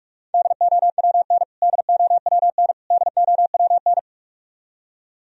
Morse code, heard as a keyed tone, about 46 wpm, 700 Hz. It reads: DOWN DOWN DOWN
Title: Top 100 Words Repeated 45wpm